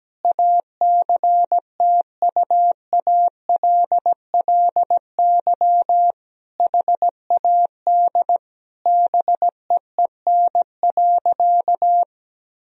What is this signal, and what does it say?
Morse code, heard as a keyed tone, about 17 wpm, 700 Hz. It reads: ACTUALLY HAD BEEN.
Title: Sentences from Top 1000 Words - Review Part 4 - 17wpm